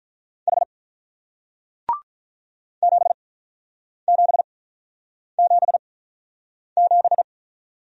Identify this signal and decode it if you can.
Morse code, heard as a keyed tone, about 47 wpm, 700 Hz. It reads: H 7 7 7 7